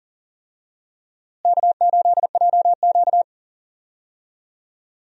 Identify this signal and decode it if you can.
Morse code, heard as a keyed tone, about 40 wpm, 700 Hz. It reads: K8JQ
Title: USA Call Signs - 4 Characters - Repeated 2x - 40wpm